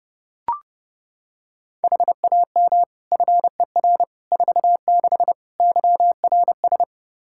Morse code, heard as a keyed tone, about 30 wpm, 700 Hz. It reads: HAM FER 46 YRS
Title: Practice QSO - Exchange 3 - More Info - 30wpm